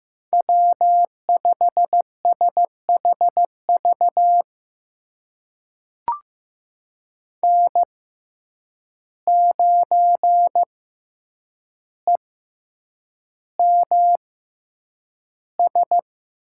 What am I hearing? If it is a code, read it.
Morse code, heard as a keyed tone, about 16 wpm, spaced as for 8 wpm, 700 Hz. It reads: W5SHV N9EMS
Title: USA Call Signs - 5 Characters - 6x Character Spacing - 15wpm